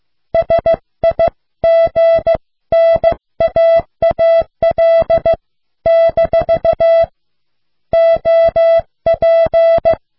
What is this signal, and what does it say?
Morse code, heard as a keyed tone, about 16 wpm, 660 Hz. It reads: SIGNAAL - OP